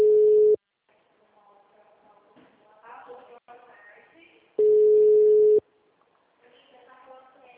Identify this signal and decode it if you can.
Ringback tone (Europe, 425 Hz), with speech.